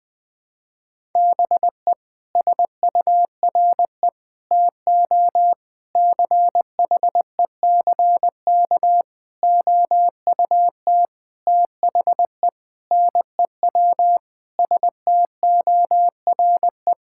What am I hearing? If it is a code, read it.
Morse code, heard as a keyed tone, about 21 wpm, 700 Hz. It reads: BE SURE TO CHECK OUT THE NEW STORE
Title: Sentences with 6-Word Phrases - Spoken Beginning - 20wpm